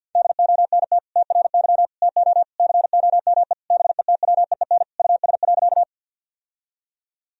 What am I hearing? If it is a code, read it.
Morse code, heard as a keyed tone, about 50 wpm, 700 Hz. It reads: DONT TRY TO COME BETWEEN US.
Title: Sentences from Top 1000 Words - Review Part 1 - 50wpm